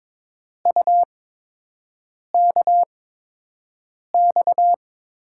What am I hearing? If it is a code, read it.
Morse code, heard as a keyed tone, about 22 wpm, 700 Hz. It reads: U K X
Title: USA Call Signs - 5 Characters - 8x Character Spacing - 22wpm